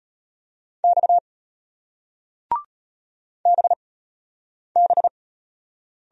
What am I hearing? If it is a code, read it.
Morse code, heard as a keyed tone, about 36 wpm, spaced as for 8 wpm, 700 Hz. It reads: X BB